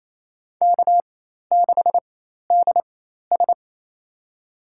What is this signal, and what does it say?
Morse code, heard as a keyed tone, about 27 wpm, spaced as for 13 wpm, 700 Hz. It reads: K6DS
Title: USA Call Signs - 4 Characters - 4x Character Spacing - 28wpm